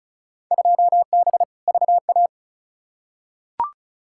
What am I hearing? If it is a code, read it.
Morse code, heard as a keyed tone, about 36 wpm, 700 Hz. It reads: 2B VA
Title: Field Day Exchanges Encore - 35wpm